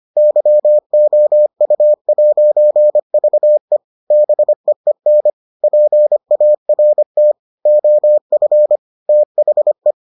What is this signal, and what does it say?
Morse code, heard as a keyed tone, about 25 wpm, 600 Hz. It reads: YOU'VE BEEN PART OF THE